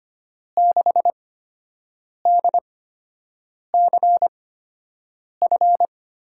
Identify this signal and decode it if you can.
Morse code, heard as a keyed tone, about 25 wpm, spaced as for 7 wpm, 700 Hz. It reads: 6DCF
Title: USA Call Signs - 8x Character Spacing - 25wpm